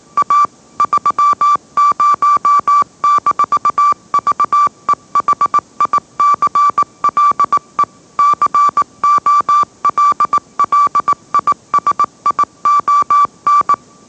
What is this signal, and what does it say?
Morse code, heard as a keyed tone, about 18 wpm, 1160 Hz. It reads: A 30-VEHICLE COLLISION
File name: mrx / mrx082 20wpm